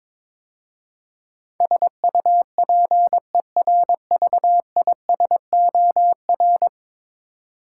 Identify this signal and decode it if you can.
Morse code, heard as a keyed tone, about 22 wpm, 700 Hz. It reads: SUPERVISOR
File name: practice / 22wpm / 10 Letter Words - First 3 Letters Given - 22wpm